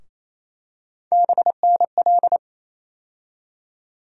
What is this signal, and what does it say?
Morse code, heard as a keyed tone, about 28 wpm, 700 Hz. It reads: BNL
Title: USA Call Sign Postfixes - 3 Letters 28wpm